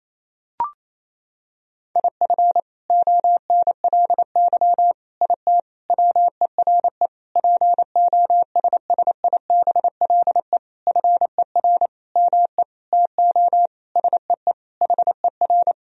Morse code, heard as a keyed tone, about 28 wpm, 700 Hz. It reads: IF ONLY IT WERE POSSIBLE FER ME TO SEE HER
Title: Sentences from Top 500 Words and Abbreviations 28wpm